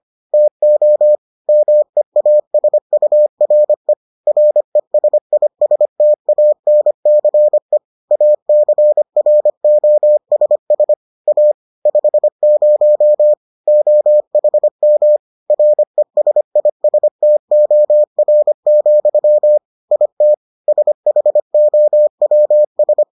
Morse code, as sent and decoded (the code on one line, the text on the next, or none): - --- / -- . .- ... ..- .-. / .-. ... .. ... - .- -. -.-. / .- -.-. .-. --- ... ... / .- / ..... ----- / --- .... -- / .-. ... .. ... - --- .-. --..-- / .. - / ... .... --- .-- ...
TO MEASURE RESISTANCE ACROSS A 50 OHM RESISTOR, IT SHOWS